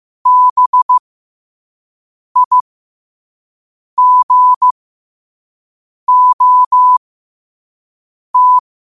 Morse code, heard as a keyed tone, about 15 wpm, spaced as for 5 wpm, 1000 Hz. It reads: BIGOT
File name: qotdfiles / 5wpm